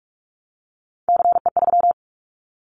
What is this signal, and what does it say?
Morse code, heard as a keyed tone, about 46 wpm, 700 Hz. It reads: CE3
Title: Callsign Prefixes for Top 100 DXCC - 45wpm